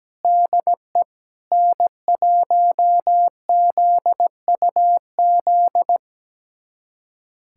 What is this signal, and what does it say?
Morse code, heard as a keyed tone, about 17 wpm, 700 Hz. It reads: DE N1ZUZ
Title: USA Callsign DE Callsign - 5 Characters - 17wpm